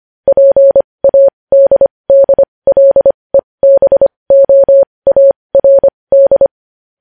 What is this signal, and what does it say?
Morse code, heard as a keyed tone, about 25 wpm, spaced as for 20 wpm, 550 Hz. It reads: PADDLEBOARD
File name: podcast / News headlines-1x-25x20